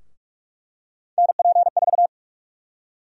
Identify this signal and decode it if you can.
Morse code, heard as a keyed tone, about 46 wpm, 700 Hz. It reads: NP4
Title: USA Call Sign Prefixes - 45wpm